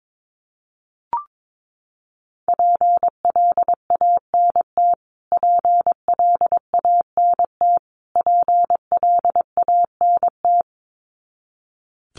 Morse code, heard as a keyed tone, about 22 wpm, 700 Hz. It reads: PLANT PLANT PLANT
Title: Top 300 Words Repeated 22wpm